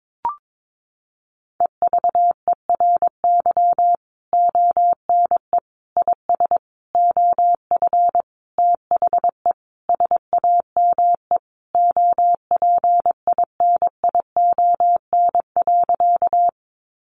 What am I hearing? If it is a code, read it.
Morse code, heard as a keyed tone, about 22 wpm, 700 Hz. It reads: EVERY ONE IS OF THE SAME OPINION.
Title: Sentences from Top 1200 Words 22wpm